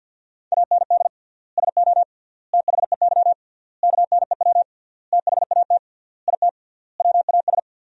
Morse code, heard as a keyed tone, about 52 wpm, 700 Hz. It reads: AND SO THEY KNEW THAT IT WAS
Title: 3x Word-Spacing - Sentences from Top 400 Words - 50wpm